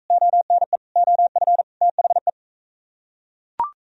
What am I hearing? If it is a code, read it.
Morse code, heard as a keyed tone, about 41 wpm, 700 Hz. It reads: ONE OF THE